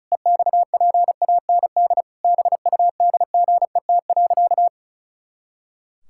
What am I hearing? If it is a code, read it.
Morse code, heard as a keyed tone, about 36 wpm, 700 Hz. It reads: EXPAND BUDGET.
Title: Pangram Challenge v2 - 35wpm